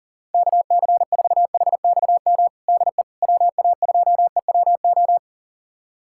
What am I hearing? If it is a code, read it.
Morse code, heard as a keyed tone, about 37 wpm, 700 Hz. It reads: KC4HXM DE WA2EWO
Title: USA Callsign DE Callsign 40wpm